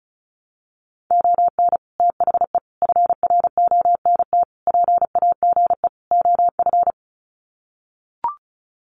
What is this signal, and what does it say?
Morse code, heard as a keyed tone, about 34 wpm, 700 Hz. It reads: ON THE FRONT PAGE OF
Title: Sets of 5 Words Encore 35wpm